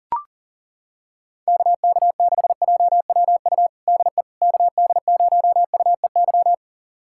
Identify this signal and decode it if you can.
Morse code, heard as a keyed tone, about 40 wpm, 700 Hz. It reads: KK6JWU DE KD0UEY
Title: USA Callsign DE Callsign 40wpm